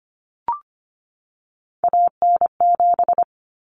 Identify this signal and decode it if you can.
Morse code, heard as a keyed tone, about 25 wpm, 700 Hz. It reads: AN7
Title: Common Cut Numbers - 3-Digit 25wpm